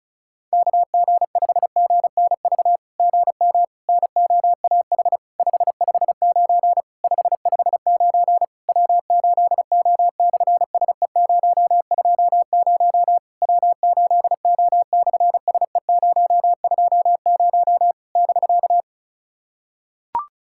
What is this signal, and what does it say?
Morse code, heard as a keyed tone, about 33 wpm, 700 Hz. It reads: KG5GNV GM NOAH 559 559 W8O/SE020 W8O/SE020 <BK>
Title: SOTA QSO Practice 35wpm